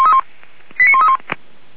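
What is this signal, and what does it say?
Five-tone selective call (ZVEI), heard as quick runs of five tones, tones of about 70 ms. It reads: 89131
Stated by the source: ZVEI five-tone call received via SDR, logged by Feuerwehr Orlamünde.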